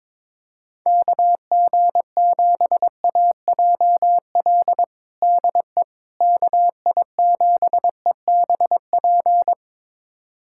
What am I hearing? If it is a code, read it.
Morse code, heard as a keyed tone, about 21 wpm, 700 Hz. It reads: KG7AJL DE KI7EBP
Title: USA Callsign DE Callsign - 6 Characters - 22wpm